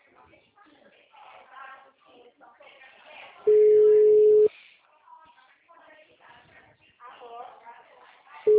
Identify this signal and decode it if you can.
Ringback tone (Europe, 425 Hz), with speech.